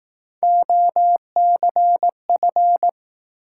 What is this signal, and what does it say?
Morse code, heard as a keyed tone, about 18 wpm, 700 Hz. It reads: OCF
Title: Speed-Racing Top 180 QSO Elements 15wpm